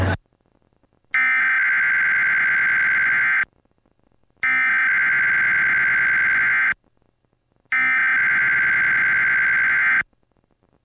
Emergency Alert System (SAME) tones (header bursts), heard as harsh buzzing bursts. It is heard from 1.1 to 10.0 s.
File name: EAS on KDVS 11-6-06